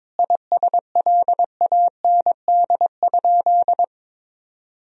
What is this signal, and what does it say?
Morse code, heard as a keyed tone, about 22 wpm, 700 Hz. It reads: ISLAND?
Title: Sentences from Top 1000 Words - Review Part 3 - 22wpm